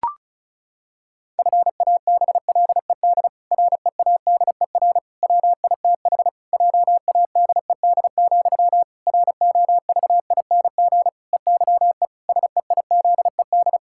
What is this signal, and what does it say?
Morse code, heard as a keyed tone, about 36 wpm, 700 Hz. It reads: FABLED READER WITH JADED, ROVING EYE SEIZED
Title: Pangram Challenge v2 - 35wpm